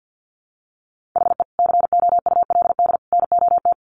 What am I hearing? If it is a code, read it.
Morse code, heard as a keyed tone, about 50 wpm, 700 Hz. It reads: HE COULD NOT
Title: Sets of 3 Words 50wpm